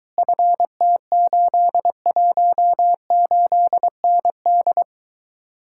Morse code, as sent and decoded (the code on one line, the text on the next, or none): ..-. - ---.. .---- ---.. -. -..
FT818ND